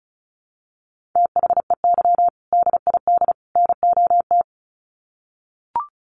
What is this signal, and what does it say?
Morse code, heard as a keyed tone, about 34 wpm, 700 Hz. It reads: THEY DID NOT
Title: Sets of 3 Words 35wpm